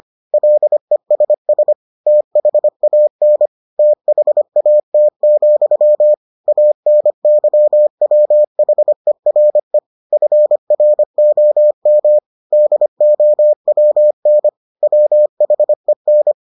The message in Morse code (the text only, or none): LESS THAN THAT, ANYWHERE FROM DOWN WHEN